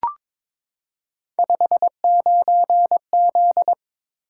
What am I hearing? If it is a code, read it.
Morse code, heard as a keyed tone, about 22 wpm, 700 Hz. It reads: 59Z
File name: practice / 22wpm / Three Letter-Number Combinations - 22wpm